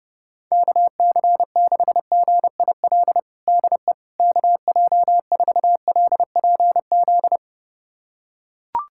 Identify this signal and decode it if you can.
Morse code, heard as a keyed tone, about 30 wpm, 700 Hz. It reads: KC6GIL DE KJ4LPZ
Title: USA Callsign DE Callsign - 6 Characters - 30wpm